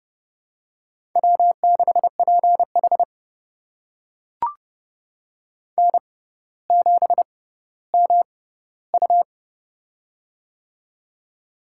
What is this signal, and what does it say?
Morse code, heard as a keyed tone, about 32 wpm, spaced as for 16 wpm, 700 Hz. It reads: W6PH N7MU
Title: USA Call Signs - 4 Characters - 6x Character Spacing - 30wpm